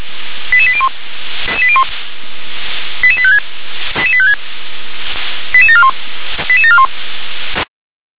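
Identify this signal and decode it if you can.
Five-tone selective call (ZVEI), heard as quick runs of five tones, tones of about 70 ms. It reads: 88801 88856 88741 88741
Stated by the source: ZVEI five-tone selective call received on an SDR.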